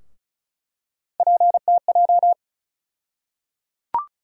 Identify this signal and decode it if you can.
Morse code, heard as a keyed tone, about 36 wpm, 700 Hz. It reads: PTJ
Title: USA Call Sign Postfixes - 3 Letters 35wpm